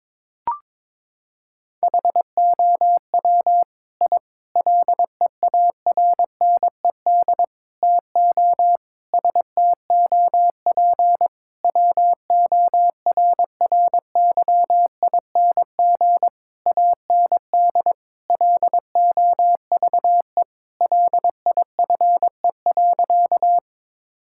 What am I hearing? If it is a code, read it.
Morse code, heard as a keyed tone, about 22 wpm, 700 Hz. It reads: HOW I LEARNED TO STOP WORRYING AND LOVE LIFE.
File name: practice / 22wpm / Sentences with 8-Word Phrases - 22wpm